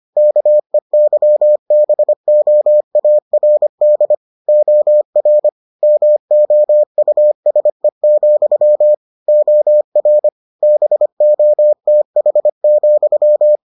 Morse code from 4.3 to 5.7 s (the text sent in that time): OR